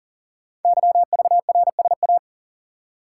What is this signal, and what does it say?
Morse code, heard as a keyed tone, about 39 wpm, 700 Hz. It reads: YVRSA
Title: Code Groups - 5 Letters - 40wpm